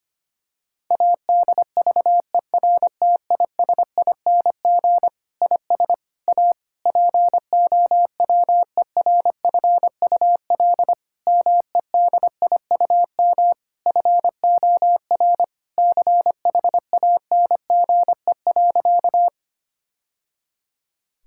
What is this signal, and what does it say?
Morse code, heard as a keyed tone, about 25 wpm, 700 Hz. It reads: ADVERTISING IS A POWERFUL MEDIUM FOR CHANGE.
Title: Sentences from Top 1000 Words - Review Part 6 - 25wpm